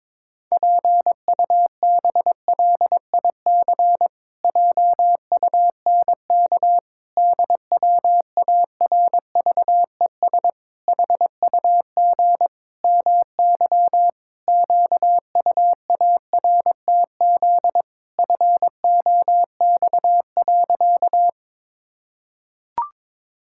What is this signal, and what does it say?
Morse code, heard as a keyed tone, about 22 wpm, 700 Hz. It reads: PUBLIC JUNK DWARVES HUG MY QUARTZ FOX.
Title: Pangram Challenge v2 - 22wpm